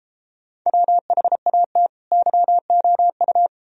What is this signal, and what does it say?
Morse code, heard as a keyed tone, about 33 wpm, 700 Hz. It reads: WHAT YOU